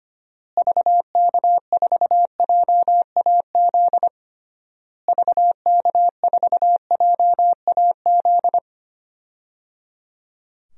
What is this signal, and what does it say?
Morse code, heard as a keyed tone, about 25 wpm, 700 Hz. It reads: VK4JAZ VK4JAZ
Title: DXCC-100 Call Signs - 6 Characters - Repeated 3x - 25wpm